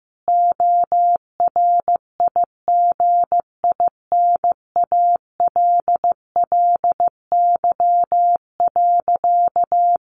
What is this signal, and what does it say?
Morse code, heard as a keyed tone, about 15 wpm, 700 Hz. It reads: ORIGINALLY.
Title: Sentences from Top 1500 Words 15wpm